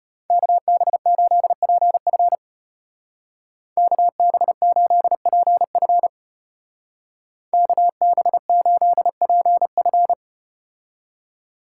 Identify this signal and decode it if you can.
Morse code, heard as a keyed tone, about 34 wpm, 700 Hz. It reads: KB8PF KB8PF KB8PF